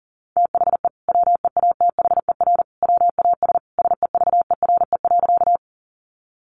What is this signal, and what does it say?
Morse code, heard as a keyed tone, about 40 wpm, 700 Hz. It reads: THE WEATHER WAS SEVERE.
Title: Sentences from Top 1400 Words 40wpm